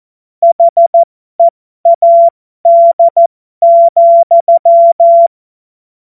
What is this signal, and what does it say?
Morse code, heard as a keyed tone, about 14 wpm, spaced as for 12 wpm, 680 Hz. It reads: HEAD,
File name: Morsekurs / Witze